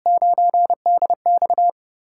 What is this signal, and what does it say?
Morse code, heard as a keyed tone, about 30 wpm, 700 Hz. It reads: 9DX
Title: USA Call Signs - 5 Characters - 6x Character Spacing - 30wpm